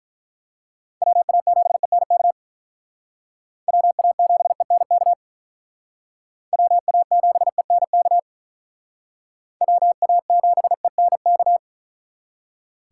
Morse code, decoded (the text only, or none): WA7ENK WA7ENK WA7ENK WA7ENK